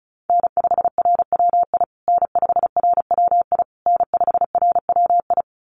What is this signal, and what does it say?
Morse code, heard as a keyed tone, about 34 wpm, 700 Hz. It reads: N5RWI N5RWI N5RWI
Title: USA Call Signs Repeated 3x 35wpm